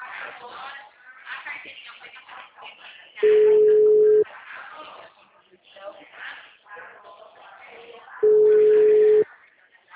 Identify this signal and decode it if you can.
Ringback tone (Europe, 425 Hz), with speech.